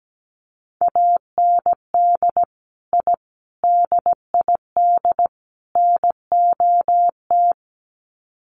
Morse code, heard as a keyed tone, about 17 wpm, 700 Hz. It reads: AND I DID NOT
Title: Sets of 4 Words 17wpm